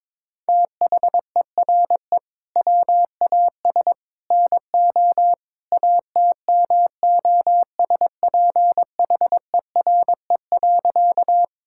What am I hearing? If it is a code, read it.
Morse code, heard as a keyed tone, about 22 wpm, 700 Hz. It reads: THERE WAS NO ATMOSPHERE.
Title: Sentences from Top 1600 Words 22wpm